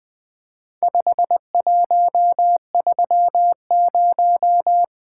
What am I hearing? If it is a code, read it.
Morse code, heard as a keyed tone, about 24 wpm, 700 Hz. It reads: 5130
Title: Speed-Racing - Four Letter-Number Combinations - 20wpm